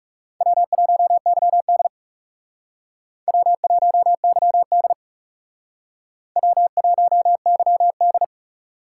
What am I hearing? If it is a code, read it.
Morse code, heard as a keyed tone, about 38 wpm, 700 Hz. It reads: W1YD W1YD W1YD